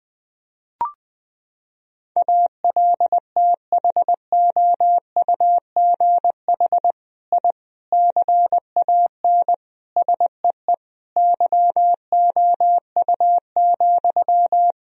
Morse code, heard as a keyed tone, about 21 wpm, 700 Hz. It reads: ALTHOUGH I CAN SEE YOU,
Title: Sentences from Top 1000 Words - Review Part 3 - 20wpm